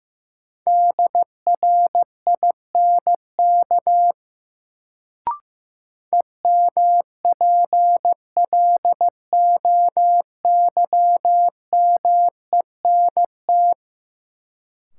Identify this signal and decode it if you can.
Morse code, heard as a keyed tone, about 15 wpm, 700 Hz. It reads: DRINK EMPLOYMENT
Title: Sentences from Top 1500 Words - New Word Drill - 15wpm